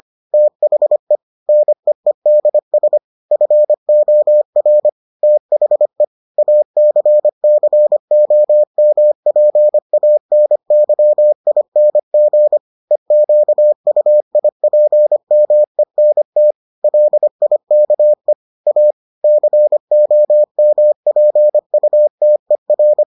Morse code, heard as a keyed tone, about 25 wpm, 600 Hz. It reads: THE NEEDS FOR THE ACCOMPANYING EQUIPMENT LIKE A COMPUTER